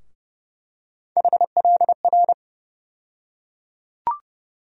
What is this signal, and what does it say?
Morse code, heard as a keyed tone, about 30 wpm, 700 Hz. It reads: HLR
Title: USA Call Sign Postfixes - 3 Letters 30wpm